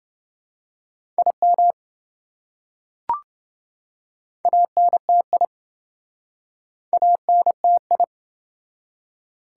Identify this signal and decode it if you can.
Morse code, heard as a keyed tone, about 27 wpm, 700 Hz. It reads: IM ANTI ANTI